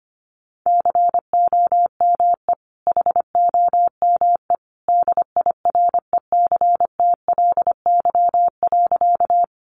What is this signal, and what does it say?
Morse code, heard as a keyed tone, about 25 wpm, 700 Hz. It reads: COME HOME DIRECTLY.
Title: Sentences from Top 800 Words 25wpm